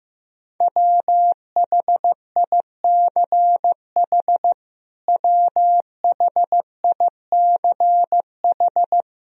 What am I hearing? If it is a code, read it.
Morse code, heard as a keyed tone, about 15 wpm, 700 Hz. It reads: WHICH WHICH
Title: Top 100 Words Repeated 15wpm